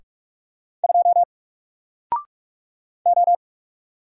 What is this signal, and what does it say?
Morse code, heard as a keyed tone, about 44 wpm, 700 Hz. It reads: 2 O
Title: Single Character (Letter:Number:Punctuation:Prosign) 45wpm